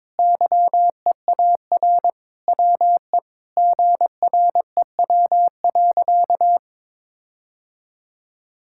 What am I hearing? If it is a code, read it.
Morse code, heard as a keyed tone, about 21 wpm, 700 Hz. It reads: YEAR WE GREW.
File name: practice / 22wpm / Sentences with 8-Word Phrases - 22wpm